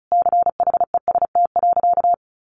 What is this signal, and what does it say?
Morse code, heard as a keyed tone, about 34 wpm, 700 Hz. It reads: CHEST.
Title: A Sound Of Thunder 35wpm